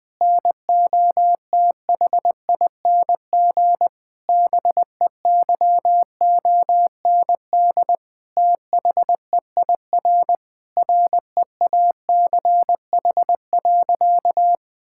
Morse code, heard as a keyed tone, about 20 wpm, 700 Hz. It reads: NOTHING BEYOND THEIR REACH.
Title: Sentences from Top 1000 Words - Review Part 6 - 20wpm